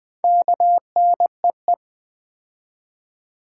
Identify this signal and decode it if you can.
Morse code, heard as a keyed tone, about 20 wpm, 700 Hz. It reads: KNEE
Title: Build A Word - 2 Letters At A Time - 4-Letter Words - 20wpm